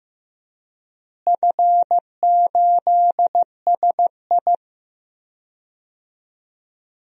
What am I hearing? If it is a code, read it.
Morse code, heard as a keyed tone, about 15 wpm, 700 Hz. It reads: F8SI
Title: DXCC-100 Call Signs - 4 Characters - Repeated 2x - 15wpm